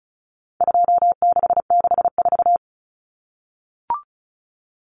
Morse code, heard as a keyed tone, about 34 wpm, 700 Hz. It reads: 2664
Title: Common Cut Numbers - 4-Digit 35wpm